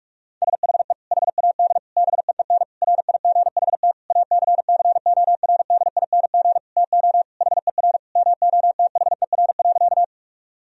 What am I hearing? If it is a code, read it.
Morse code, heard as a keyed tone, about 44 wpm, 700 Hz. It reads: SHE HAD BEEN RIGHT ACCORDING TO HER MOTHER.
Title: Sentences from Top 1000 Words - Review Part 4 - 45wpm